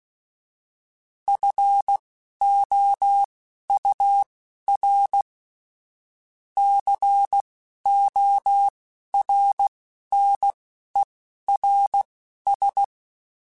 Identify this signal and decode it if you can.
Morse code, heard as a keyed tone, about 16 wpm, spaced as for 11 wpm, 800 Hz. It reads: FOUR CORNERS